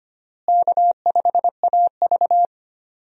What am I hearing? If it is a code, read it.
Morse code, heard as a keyed tone, about 25 wpm, 700 Hz. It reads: K5AV
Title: USA Call Signs - 4 Characters - Repeated 3x - 25wpm